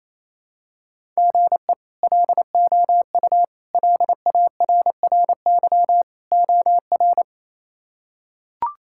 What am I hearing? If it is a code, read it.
Morse code, heard as a keyed tone, about 28 wpm, 700 Hz. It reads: GE LOU LARRY OR
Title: K1USN SST QSO Practice - 28wpm